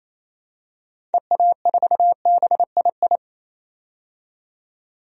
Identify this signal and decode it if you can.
Morse code, heard as a keyed tone, about 28 wpm, 700 Hz. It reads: EA4BII
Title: DXCC-100 Call Signs - 6 Characters - 28wpm